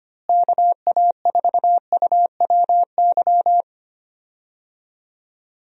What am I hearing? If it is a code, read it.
Morse code, heard as a keyed tone, about 25 wpm, 700 Hz. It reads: KA4UWY
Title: USA Call Signs - 6 Characters - Repeated 3x - 25wpm